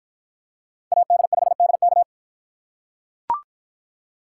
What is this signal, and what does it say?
Morse code, heard as a keyed tone, about 45 wpm, 700 Hz. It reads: AD5DK